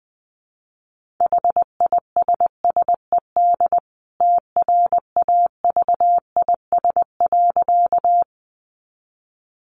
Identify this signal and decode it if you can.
Morse code, heard as a keyed tone, about 20 wpm, 700 Hz. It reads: HISSED TRAVIS.
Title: A Sound Of Thunder 20wpm